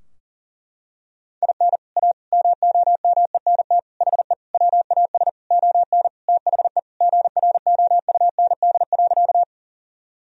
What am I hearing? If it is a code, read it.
Morse code, heard as a keyed tone, about 40 wpm, 700 Hz. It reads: IN A MOMENT HE WAS ON THE GROUND.
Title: Sentences from Top 1000 Words - Review Part 2 - 40wpm